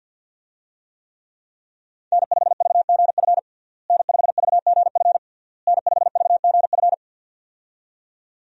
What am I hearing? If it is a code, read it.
Morse code, heard as a keyed tone, about 52 wpm, 700 Hz. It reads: N5VGF N5VGF N5VGF
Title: USA Call Signs - 5 Characters - Repeated 3x - 50wpm